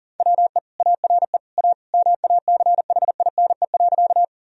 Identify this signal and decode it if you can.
Morse code, heard as a keyed tone, about 40 wpm, 700 Hz. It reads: WE ARE A MACHINE.
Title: Sentences from Top 1000 Words - Review Part 2 - 40wpm